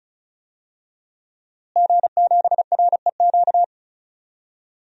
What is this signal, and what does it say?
Morse code, heard as a keyed tone, about 34 wpm, 700 Hz. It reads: G7REQ
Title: DXCC-100 Call Signs - 5 Characters - Repeated 3x - 35wpm